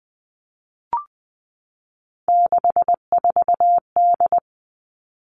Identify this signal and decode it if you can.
Morse code, heard as a keyed tone, about 20 wpm, 700 Hz. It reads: 64D
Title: Three Challenging Letter-Number Combinations 20wpm